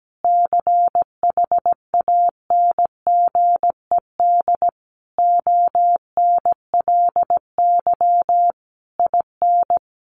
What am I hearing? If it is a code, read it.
Morse code, heard as a keyed tone, about 17 wpm, 700 Hz. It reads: CHANGED ONLY IN
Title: A Sound Of Thunder 17wpm